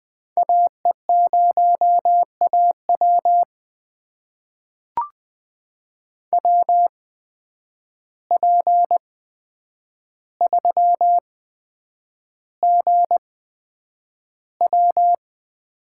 Morse code, heard as a keyed tone, about 20 wpm, spaced as for 9 wpm, 700 Hz. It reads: AE0AW WP3GW
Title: USA Call Signs - 5 Characters - 8x Character Spacing - 20wpm